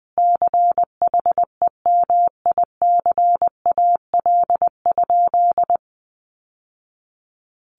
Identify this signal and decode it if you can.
Morse code, heard as a keyed tone, about 20 wpm, 700 Hz. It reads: CHEMICAL?
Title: Sentences from Top 1200 Words 20wpm